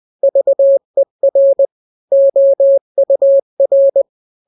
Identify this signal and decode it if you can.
Morse code, heard as a keyed tone, about 20 wpm, 550 Hz. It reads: VER OUR
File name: n-grams / 3-grams 20 18 WPM